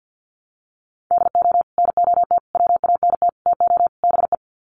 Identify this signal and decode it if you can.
Morse code, heard as a keyed tone, about 50 wpm, 700 Hz. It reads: DO NOT WANT TO BE
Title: Sets of 5 Words 50wpm